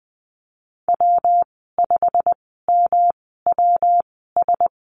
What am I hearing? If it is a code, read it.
Morse code, heard as a keyed tone, about 19 wpm, spaced as for 14 wpm, 700 Hz. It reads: W5MWS
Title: 128 Lesson 21g - Number 6 - Call Signs 20wpm